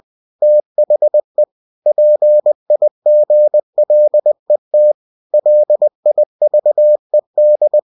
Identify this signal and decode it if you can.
Morse code, heard as a keyed tone, about 20 wpm, 600 Hz. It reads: THE PIGLET LIVED
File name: cw-books / winnie-the-pooh / winnie-20wpm-page-0020